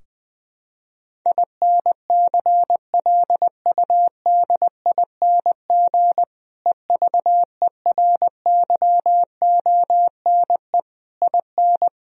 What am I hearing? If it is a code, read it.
Morse code, heard as a keyed tone, about 20 wpm, 700 Hz. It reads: INCLUDING EVERYONE IN
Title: Sentences from Top 1000 Words - Review Part 4 - 20wpm